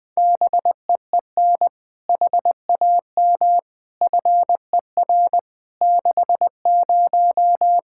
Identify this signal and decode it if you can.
Morse code, heard as a keyed tone, about 20 wpm, 700 Hz. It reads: BEEN HAM FER 60
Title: Practice QSO - Exchange 3 - More Info - 20wpm